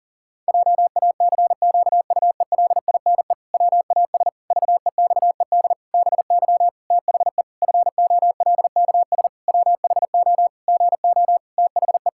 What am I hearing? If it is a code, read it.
Morse code, heard as a keyed tone, about 37 wpm, 700 Hz. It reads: JACQUELINE WAS VEXED BY THE FOLKS WHO GO THE
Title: Pangram Challenge v2 - 40wpm